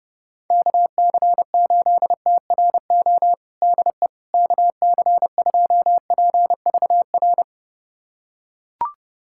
Morse code, heard as a keyed tone, about 30 wpm, 700 Hz. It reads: KC8TRO DE KC2PVR
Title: USA Callsign DE Callsign - 6 Characters - 30wpm